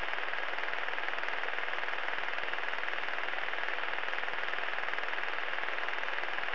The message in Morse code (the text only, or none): none